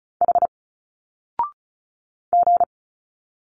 Morse code, heard as a keyed tone, about 36 wpm, 700 Hz. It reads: H G
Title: Single Letters 35wpm